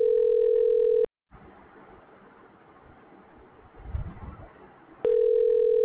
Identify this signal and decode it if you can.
Ringback tone (North America), 440 + 480 Hz.